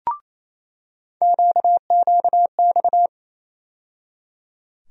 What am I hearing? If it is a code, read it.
Morse code, heard as a keyed tone, about 28 wpm, 700 Hz. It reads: QQX
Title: USA Call Sign Postfixes - 3 Letters 28wpm